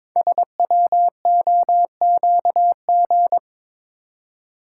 Morse code, heard as a keyed tone, about 22 wpm, 690 Hz. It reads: SWOQG
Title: Code Groups - 5 Letters - 22wpm